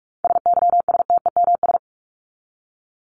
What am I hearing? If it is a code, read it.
Morse code, heard as a keyed tone, about 44 wpm, 700 Hz. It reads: SYSTEMS
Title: ES:S Suffix - Top 100 Words with Base Words 45wpm